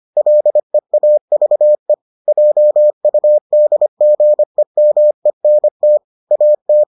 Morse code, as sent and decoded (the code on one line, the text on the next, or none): .-.. .- ...- . / .--- ..- -.. --. -- . -. - / .- -
LEAVE JUDGEMENT AT